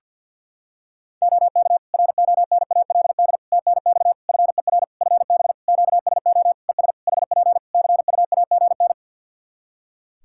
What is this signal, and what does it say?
Morse code, heard as a keyed tone, about 52 wpm, 700 Hz. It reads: OK RONALD TNX FER FB QSO ES HP CUAGN
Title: Practice QSO - Exchange 4 - Ending - 50wpm